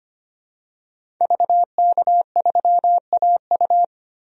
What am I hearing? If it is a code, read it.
Morse code, heard as a keyed tone, about 25 wpm, 690 Hz. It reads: VK3AU
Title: DXCC-100 Call Signs - 5 Characters - 25wpm